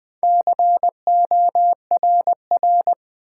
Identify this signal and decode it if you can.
Morse code, heard as a keyed tone, about 21 wpm, 700 Hz. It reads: CORR
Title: Build A Word - 2 Letters At A Time - 8-Letter Words - 20wpm